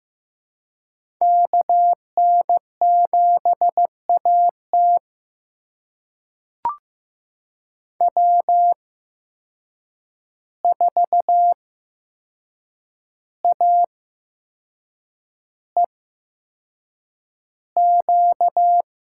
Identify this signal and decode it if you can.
Morse code, heard as a keyed tone, about 15 wpm, spaced as for 6 wpm, 700 Hz. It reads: KN7AT W4AEQ
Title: USA Call Signs - 5 Characters - 8x Character Spacing - 15wpm